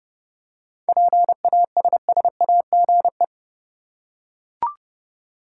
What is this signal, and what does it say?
Morse code, heard as a keyed tone, about 30 wpm, 700 Hz. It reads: PASSAGE
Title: Seven Letter Words - 30wpm